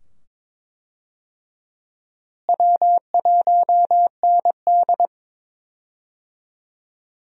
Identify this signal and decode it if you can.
Morse code, heard as a keyed tone, about 21 wpm, 700 Hz. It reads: W1ND
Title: USA Call Signs - 4 Characters - Repeated 2x - 22wpm